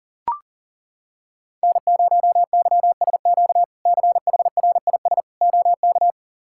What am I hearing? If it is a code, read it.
Morse code, heard as a keyed tone, about 40 wpm, 700 Hz. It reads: N0YSQ CHRIS OK
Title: K1USN SST QSO Practice - 40wpm